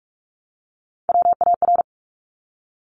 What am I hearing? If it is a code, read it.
Morse code, heard as a keyed tone, about 46 wpm, 700 Hz. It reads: WAR
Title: Top Three Letter Combinations in Words 45wpm